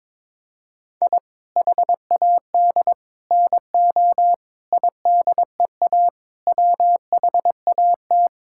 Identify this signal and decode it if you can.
Morse code, heard as a keyed tone, about 22 wpm, 700 Hz. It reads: I HAD NO IDEA WHAT
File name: practice / 22wpm / Sets of 8 Words - 22wpm